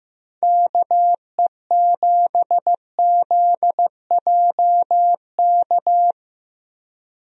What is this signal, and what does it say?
Morse code, heard as a keyed tone, about 15 wpm, 700 Hz. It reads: KE7ZJK
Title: USA Call Signs - 6 Characters - Repeated 2x - 15wpm